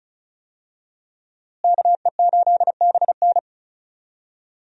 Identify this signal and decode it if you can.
Morse code, heard as a keyed tone, about 34 wpm, 700 Hz. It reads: KE8BN
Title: USA Call Signs - 5 Characters - Repeated 3x - 35wpm